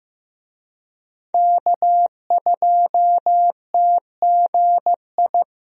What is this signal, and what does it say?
Morse code, heard as a keyed tone, about 16 wpm, 700 Hz. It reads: K2TGI
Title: USA Call Signs - 5 Characters - Repeated 3x - 15wpm